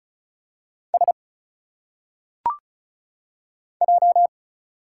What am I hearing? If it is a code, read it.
Morse code, heard as a keyed tone, about 35 wpm, 700 Hz. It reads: S J
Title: Single Letters - Phonetic Pronunciation 35wpm